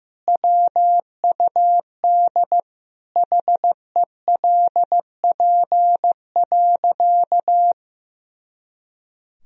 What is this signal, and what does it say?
Morse code, heard as a keyed tone, about 16 wpm, 700 Hz. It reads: WUD HELP.
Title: Sentences from Top 200 Words and Abbreviations 15wpm